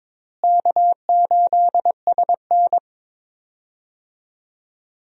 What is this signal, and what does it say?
Morse code, heard as a keyed tone, about 22 wpm, 700 Hz. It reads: K8SN
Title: USA Call Signs - 4 Characters - Repeated 2x - 22wpm